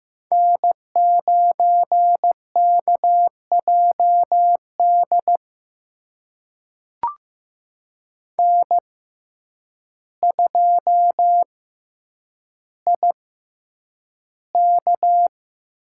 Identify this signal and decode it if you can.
Morse code, heard as a keyed tone, about 15 wpm, spaced as for 8 wpm, 700 Hz. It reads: N9KJD N2IK
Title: USA Call Signs - 6x Character Spacing - 15wpm